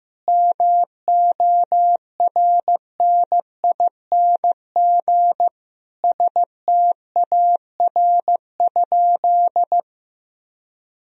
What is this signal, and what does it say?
Morse code, heard as a keyed tone, about 15 wpm, 700 Hz. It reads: MORNING STAR?
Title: Sentences from Top 1000 Words - Review Part 2 - 15wpm